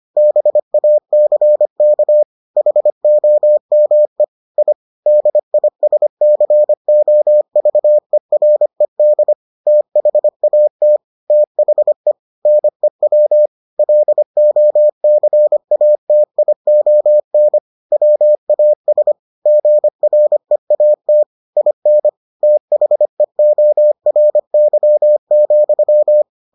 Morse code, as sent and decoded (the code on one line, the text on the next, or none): -... .- -.-. -.- / .... --- -- . / .. / -.. .. ... -.-. --- ...- . .-. -.. / - .... .- - / - .... / -. .-- / .-.. --- -.-. .- - .. --- -. / .-- .- ... / --. .-. .- - / .. -. / - .... --- .-. -.-- --..--
BACK HOME I DISCOVERED THAT THE NEW LOCATION WAS GREAT IN THEORY,